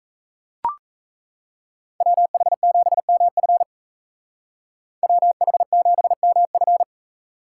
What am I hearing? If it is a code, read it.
Morse code, heard as a keyed tone, about 36 wpm, 700 Hz. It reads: WH7MF WH7MF